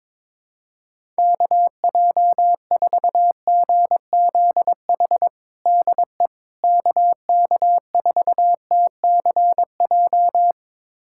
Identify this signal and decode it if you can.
Morse code, heard as a keyed tone, about 22 wpm, 700 Hz. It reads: KJ4GZH DE KK4TCJ
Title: USA Callsign DE Callsign - 6 Characters - 22wpm